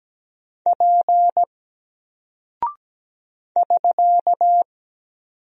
Morse code, heard as a keyed tone, about 17 wpm, 700 Hz. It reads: P <SK>
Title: Single Character (Letter:Number:Punctuation:Prosign) 17wpm